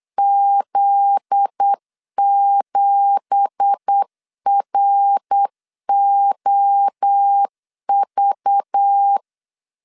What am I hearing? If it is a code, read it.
Morse code, heard as a keyed tone, about 8 wpm, 800 Hz. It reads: Z7ROV
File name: 8wpm